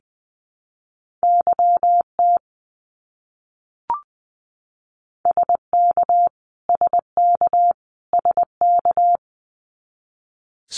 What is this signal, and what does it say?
Morse code, heard as a keyed tone, about 20 wpm, 700 Hz. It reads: YT SK SK SK
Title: Canadian Provinces and Territories - Repeated 3x - 20wpm